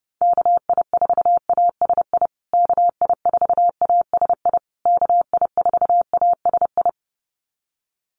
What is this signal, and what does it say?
Morse code, heard as a keyed tone, about 30 wpm, 700 Hz. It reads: KI4ASI KI4ASI KI4ASI
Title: USA Call Signs Repeated 3x 30wpm